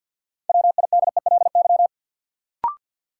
Morse code, heard as a keyed tone, about 53 wpm, 700 Hz. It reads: WIDELY
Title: Sentences from Top 1200 Words - New Word Drill - 50wpm